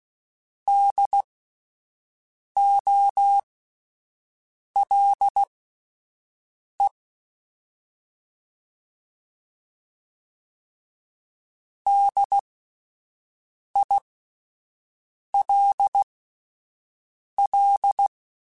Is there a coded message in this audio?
Yes: Morse code from 0.7 to 18.1 s, heard as a keyed tone, about 16 wpm, spaced as for 5 wpm, 800 Hz. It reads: DOLE DILL